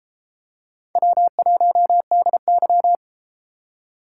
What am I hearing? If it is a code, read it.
Morse code, heard as a keyed tone, about 33 wpm, 700 Hz. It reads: W1DY